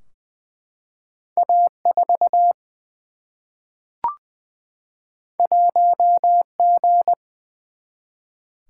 Morse code, heard as a keyed tone, about 20 wpm, 700 Hz. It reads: A4 1G
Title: Single-Letter Single-Number 20wpm